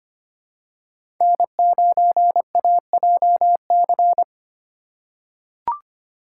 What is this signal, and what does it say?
Morse code, heard as a keyed tone, about 25 wpm, 700 Hz. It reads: N9AJC
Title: USA Call Signs - 4x Character Spacing - 25wpm